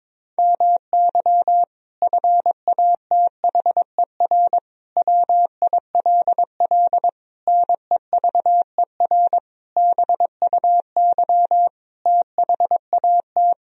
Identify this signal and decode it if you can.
Morse code, heard as a keyed tone, about 22 wpm, 700 Hz. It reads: MY FATHER WILL NEVER BUY THAT
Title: Sentences from Top 1000 Words - Review Part 4 - 22wpm